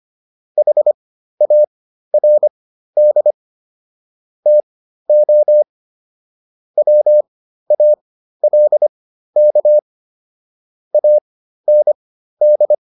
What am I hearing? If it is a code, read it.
Morse code, heard as a keyed tone, about 25 wpm, spaced as for 13 wpm, 600 Hz. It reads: HARD TO WALK AND